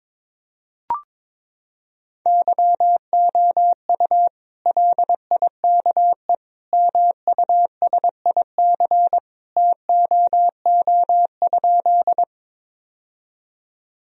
Morse code, heard as a keyed tone, about 21 wpm, 700 Hz. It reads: YOU LIKE MUSIC TOO?
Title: Sentences from Top 1000 Words - Review Part 1 - 22wpm